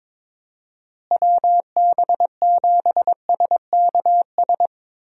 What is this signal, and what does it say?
Morse code, heard as a keyed tone, about 22 wpm, 700 Hz. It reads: WB7SKS
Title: USA Call Signs - 6 Characters - Repeated 3x - 22wpm